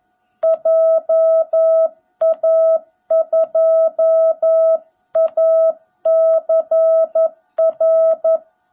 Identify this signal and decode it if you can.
Morse code, heard as a keyed tone, about 11 wpm, 660 Hz. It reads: JA2ACR
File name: jh8cbh / 2021cw / cw17b